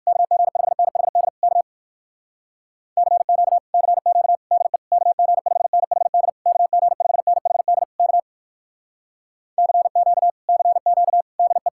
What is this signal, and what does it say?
Morse code, heard as a keyed tone, about 58 wpm, 700 Hz. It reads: KG5NHD K CQ CQ DE KG5NHD KG5NHD K CQ CQ DE